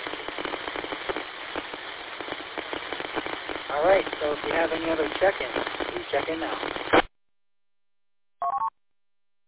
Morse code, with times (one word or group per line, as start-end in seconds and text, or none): none